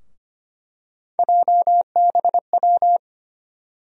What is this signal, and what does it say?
Morse code, heard as a keyed tone, about 25 wpm, 700 Hz. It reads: JBW
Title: USA Call Sign Postfixes - 3 Letters 25wpm